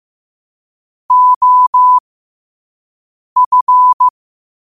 Morse code, heard as a keyed tone, about 15 wpm, spaced as for 5 wpm, 1000 Hz. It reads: OF
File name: qotdfiles / 5wpm